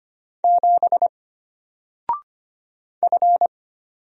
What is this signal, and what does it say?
Morse code, heard as a keyed tone, about 25 wpm, 700 Hz. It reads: 7 F
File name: Single-Letter-Number-25wpm